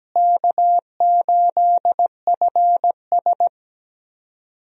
Morse code, heard as a keyed tone, about 17 wpm, 700 Hz. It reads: K8FS
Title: USA Call Signs - 4 Characters - Repeated 3x - 17wpm